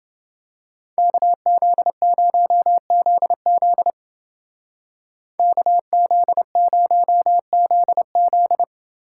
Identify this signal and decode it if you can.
Morse code, heard as a keyed tone, about 29 wpm, 700 Hz. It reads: KZ0ZZ KZ0ZZ